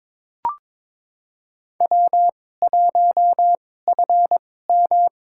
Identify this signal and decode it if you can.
Morse code, heard as a keyed tone, about 22 wpm, spaced as for 16 wpm, 700 Hz. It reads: W1FM
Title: USA Call Signs - 4 Characters - 2x Character Spacing - 22wpm